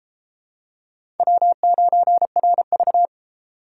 Morse code, heard as a keyed tone, about 30 wpm, 700 Hz. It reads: W9RV